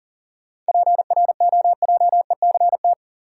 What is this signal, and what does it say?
Morse code, heard as a keyed tone, about 40 wpm, 700 Hz. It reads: PROJECT
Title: Sentences from Top 1400 Words - New Word Drill - 40wpm